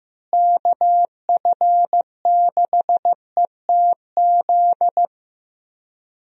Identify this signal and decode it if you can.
Morse code, heard as a keyed tone, about 15 wpm, 700 Hz. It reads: KF6ETZ
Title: USA Call Signs - 6 Characters - Repeated 3x - 15wpm